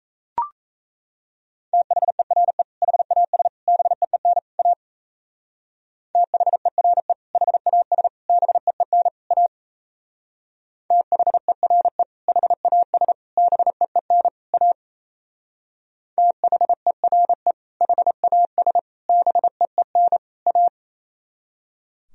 Morse code, heard as a keyed tone, about 34 wpm, 700 Hz. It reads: THERE HAS BEEN A THERE HAS BEEN A THERE HAS BEEN A THERE HAS BEEN A
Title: Speed-Racing - Sets of 4 Words 28wpm